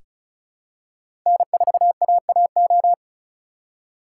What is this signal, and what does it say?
Morse code, heard as a keyed tone, about 34 wpm, 700 Hz. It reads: N4AAO
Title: USA Call Signs - 2x Character Spacing - 35wpm